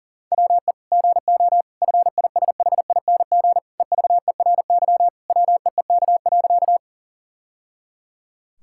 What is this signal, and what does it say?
Morse code, heard as a keyed tone, about 40 wpm, 700 Hz. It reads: WE GO FISHING EVERY WEEK.
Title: Sentences from Top 1000 Words - Review Part 5 - 40wpm